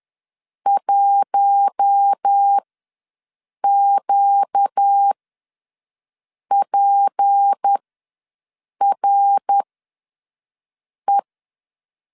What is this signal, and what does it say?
Morse code, heard as a keyed tone, about 10 wpm, spaced as for 5 wpm, 800 Hz. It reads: 1QPRE
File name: LEZIONI / 6 PKX0 / 6 PK ws 10wpm 3745